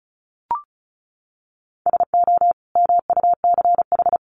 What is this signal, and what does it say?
Morse code, heard as a keyed tone, about 34 wpm, 700 Hz. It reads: SO MUCH
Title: Sets of 2 Words 35wpm